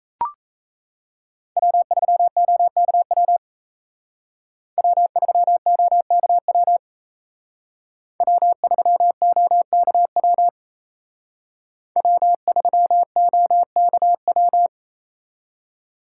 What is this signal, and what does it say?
Morse code, heard as a keyed tone, about 33 wpm, 700 Hz. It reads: W3OKW W3OKW W3OKW W3OKW
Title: Speed Racing - USA Call Signs 28wpm